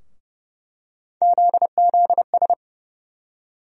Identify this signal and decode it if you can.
Morse code, heard as a keyed tone, about 30 wpm, 700 Hz. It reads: ZZS
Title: USA Call Sign Postfixes - 3 Letters 30wpm